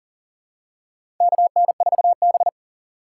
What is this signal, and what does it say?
Morse code, heard as a keyed tone, about 40 wpm, 700 Hz. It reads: KN4B
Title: USA Call Signs - 4 Characters - 3x Character Spacing - 40wpm